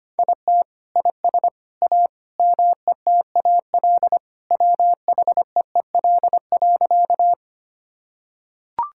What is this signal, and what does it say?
Morse code, heard as a keyed tone, about 25 wpm, 700 Hz. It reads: IT IS A METAL WHEEL.
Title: Sentences from Top 1000 Words - Review Part 6 - 25wpm